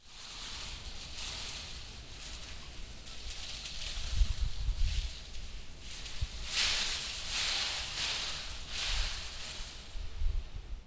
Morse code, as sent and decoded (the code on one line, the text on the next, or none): none